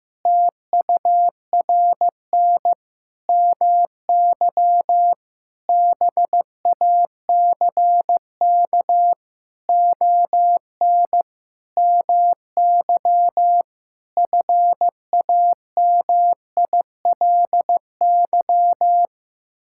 Morse code, heard as a keyed tone, about 15 wpm, 700 Hz. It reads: TURN MY BACK ON MY FAMILY
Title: Sentences from Top 1000 Words - Review Part 6 - 15wpm